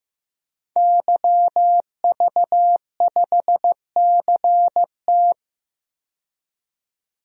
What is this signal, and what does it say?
Morse code, heard as a keyed tone, about 15 wpm, 700 Hz. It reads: YV5CT
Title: DXCC-100 Call Signs - 5 Characters - 15wpm